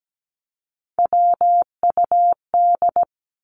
Morse code, heard as a keyed tone, about 17 wpm, 700 Hz. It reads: WUD
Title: Abbreviations 17wpm